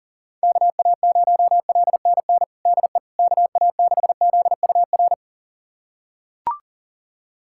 Morse code, heard as a keyed tone, about 40 wpm, 700 Hz. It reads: KA0LNN DE KA6ZUR
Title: USA Callsign DE Callsign - 6 Characters - 40wpm